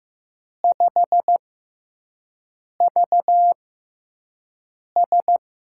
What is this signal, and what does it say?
Morse code, heard as a keyed tone, about 15 wpm, spaced as for 5 wpm, 700 Hz. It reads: 5VS
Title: USA Call Signs - 6x Character Spacing - 15wpm